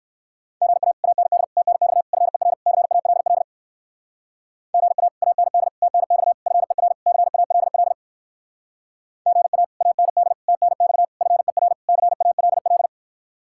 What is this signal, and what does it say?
Morse code, heard as a keyed tone, about 58 wpm, 700 Hz. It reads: GA AND TNX FER CALL GA AND TNX FER CALL GA AND TNX FER CALL